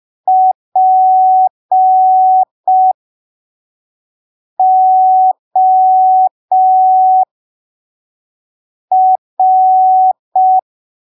Morse code, heard as a keyed tone, about 5 wpm, spaced as for 3 wpm, 750 Hz. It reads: POR
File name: arquivos / texto / texto 18 05ppm